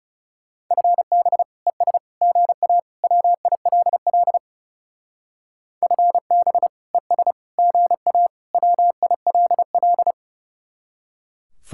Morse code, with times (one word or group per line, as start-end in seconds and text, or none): 0.7-1.4 FB
1.7-2.0 ES
2.2-2.8 GA
3.0-4.4 WILL
5.8-6.7 FB
6.9-7.3 ES
7.6-8.3 GA
8.5-10.1 WILL